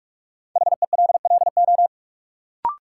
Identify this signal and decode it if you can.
Morse code, heard as a keyed tone, about 46 wpm, 700 Hz. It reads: HELLO
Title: Sentences from Top 1100 Words - New Word Drill - 45wpm